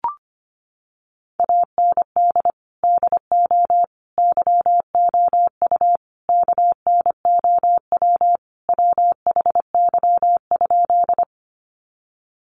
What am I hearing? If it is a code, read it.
Morse code, heard as a keyed tone, about 25 wpm, 700 Hz. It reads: AND DO YOU KNOW WHY?
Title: Sentences from Top 200 Words 25wpm